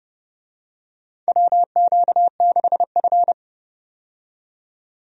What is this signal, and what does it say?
Morse code, heard as a keyed tone, about 32 wpm, 700 Hz. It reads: WQ6F
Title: USA Call Signs - 4 Characters - 30wpm